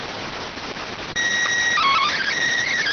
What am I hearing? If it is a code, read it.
SSTV (slow-scan television), PD 290.